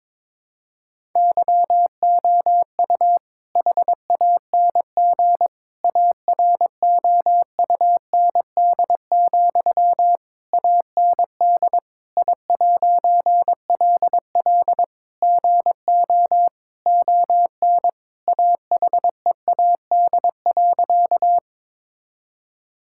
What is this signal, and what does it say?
Morse code, heard as a keyed tone, about 22 wpm, 700 Hz. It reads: YOU HANG AROUND, AND I'LL GO ON AHEAD.
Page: morse-code-ninja-practice-22wpm